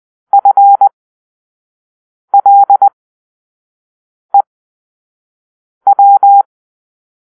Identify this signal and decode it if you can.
Morse code, heard as a keyed tone, about 20 wpm, spaced as for 5 wpm, 800 Hz. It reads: FLEW
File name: arrl / Grp-4 Drill-4